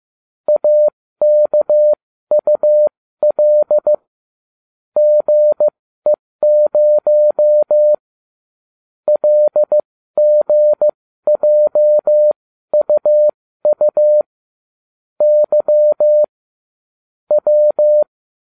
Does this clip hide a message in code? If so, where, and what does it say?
Yes: Morse code from 0.5 to 18.0 s, heard as a keyed tone, about 15 wpm, spaced as for 13 wpm, 600 Hz. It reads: AKUL GE0 LGJUU Y W